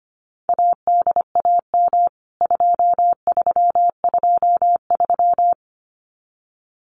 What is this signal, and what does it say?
Morse code, heard as a keyed tone, about 25 wpm, 700 Hz. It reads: ADAM 2323
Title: CWT Exchange 25wpm